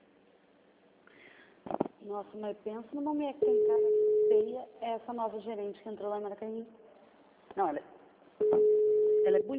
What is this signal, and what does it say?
Ringback tone (Europe, 425 Hz), with speech.